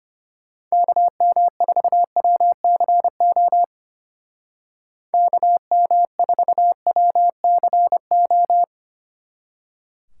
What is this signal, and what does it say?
Morse code, heard as a keyed tone, about 30 wpm, 700 Hz. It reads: KM4WCO KM4WCO
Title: Speed Racing - USA 2x3 Call Signs 25wpm